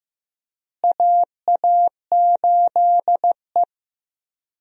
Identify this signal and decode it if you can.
Morse code, heard as a keyed tone, about 15 wpm, 700 Hz. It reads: AA8E
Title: USA Call Signs - 4 Characters - Repeated 2x - 15wpm